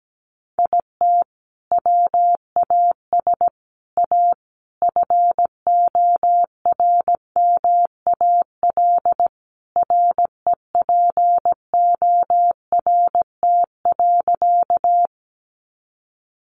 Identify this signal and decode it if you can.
Morse code, heard as a keyed tone, about 17 wpm, 700 Hz. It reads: IT WAS A FORMAL REPORT.
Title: Sentences from Top 1300 Words 17wpm